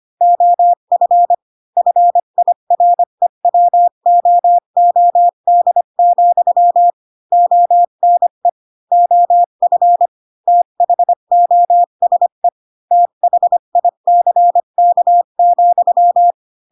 Morse code, heard as a keyed tone, about 25 wpm, 700 Hz. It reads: OF FIREWOOD, ONE OF THOSE THICK,
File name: Pinocchio / Pinocchio ch1 20 25 ik1hge